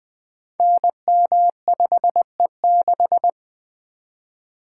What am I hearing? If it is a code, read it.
Morse code, heard as a keyed tone, about 20 wpm, 700 Hz. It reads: NM5E6
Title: Code Groups - 5 Letter:Numbers - 20wpm